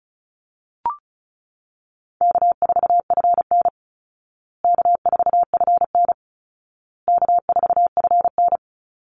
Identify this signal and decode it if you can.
Morse code, heard as a keyed tone, about 34 wpm, 700 Hz. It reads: K4FN K4FN K4FN
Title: Call Signs Repeated 2x - 35wpm